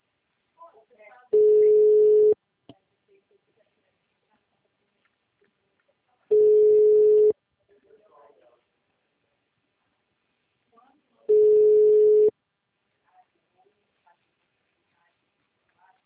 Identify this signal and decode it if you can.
Ringback tone (Europe, 425 Hz), with speech.